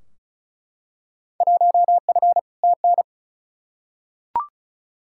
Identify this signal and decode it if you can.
Morse code, heard as a keyed tone, about 36 wpm, 700 Hz. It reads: 1F TN
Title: Field Day Exchanges - 35wpm